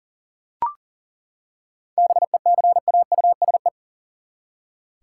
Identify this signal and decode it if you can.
Morse code, heard as a keyed tone, about 40 wpm, 700 Hz. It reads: BECAUSE
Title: Seven Letter Words - 40wpm